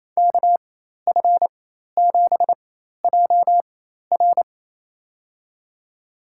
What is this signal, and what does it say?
Morse code, heard as a keyed tone, about 28 wpm, spaced as for 13 wpm, 700 Hz. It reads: KF7JR
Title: USA Call Signs - 5 Characters - 4x Character Spacing - 28wpm